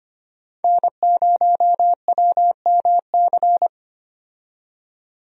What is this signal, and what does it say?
Morse code, heard as a keyed tone, about 25 wpm, 700 Hz. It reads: N0WMC
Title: USA Call Signs - 5 Characters - 25wpm